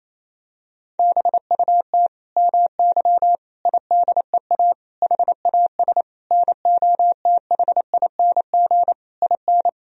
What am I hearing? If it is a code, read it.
Morse code, heard as a keyed tone, about 28 wpm, 700 Hz. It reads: BUT MY IDEA HAS NOTHING IN
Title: Sentences from Top 1000 Words - Review Part 3 - 28wpm